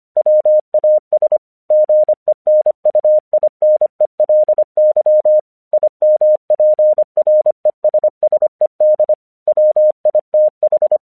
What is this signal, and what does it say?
Morse code, heard as a keyed tone, about 25 wpm, 610 Hz. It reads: WAS GENUINELY IMPRESSED WITH
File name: 1 English 25wpm / AQC26